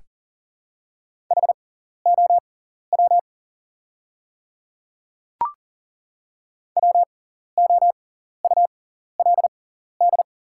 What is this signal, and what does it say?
Morse code, heard as a keyed tone, about 40 wpm, spaced as for 14 wpm, 700 Hz. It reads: HOW WOULD
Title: 6x Character Spacing - Top 100 Words - 40wpm